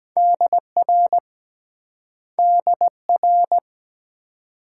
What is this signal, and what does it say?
Morse code, heard as a keyed tone, about 22 wpm, 700 Hz. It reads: DR DR